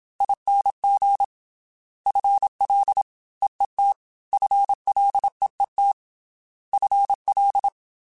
Morse code, heard as a keyed tone, about 27 wpm, 800 Hz. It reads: ING FL EET FLEET FL
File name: FL / FL mp3s / Fast FL @25